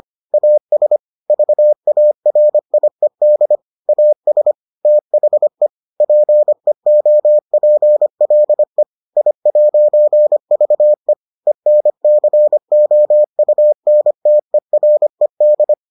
Morse code, heard as a keyed tone, about 25 wpm, 600 Hz. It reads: AS VARIED AS THE PEOPLE I'VE ENCOUNTERED